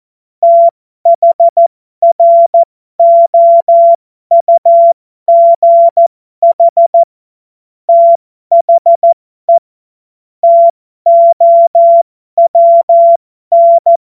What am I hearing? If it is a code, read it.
Morse code, heard as a keyed tone, about 14 wpm, spaced as for 12 wpm, 680 Hz. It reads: THROUGH THE TOWN
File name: Morsekurs / Witze